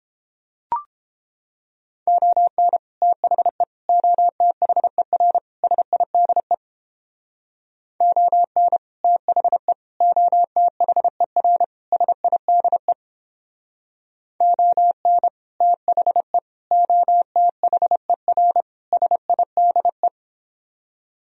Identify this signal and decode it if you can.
Morse code, heard as a keyed tone, about 29 wpm, 700 Hz. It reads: ON THE OTHER SIDE ON THE OTHER SIDE ON THE OTHER SIDE